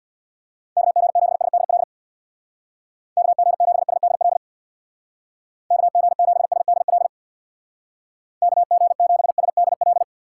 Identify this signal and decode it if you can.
Morse code, heard as a keyed tone, about 61 wpm, 700 Hz. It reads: KG7SDL KG7SDL KG7SDL KG7SDL
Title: Speed Racing - USA Call Signs 50wpm